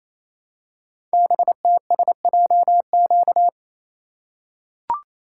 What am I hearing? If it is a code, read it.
Morse code, heard as a keyed tone, about 29 wpm, 700 Hz. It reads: BTSJQ
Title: Code Groups - 5 Letters - 28wpm